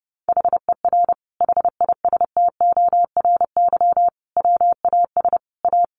Morse code, heard as a keyed tone, about 30 wpm, 700 Hz. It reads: HER HISTORY WAS A
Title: Sentences from Top 1300 Words 30wpm